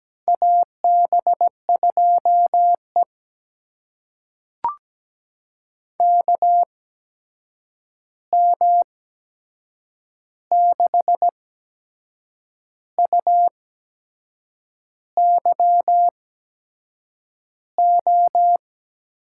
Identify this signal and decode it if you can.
Morse code, heard as a keyed tone, about 17 wpm, spaced as for 6 wpm, 700 Hz. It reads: AB2E KM6UYO
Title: USA Call Signs - 8x Character Spacing - 17wpm